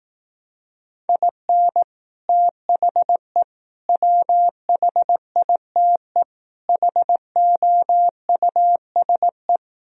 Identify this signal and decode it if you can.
Morse code, heard as a keyed tone, about 18 wpm, 700 Hz. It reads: IN THE WHITE HOUSE
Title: Speed-Racing - Sets of 4 Words 15wpm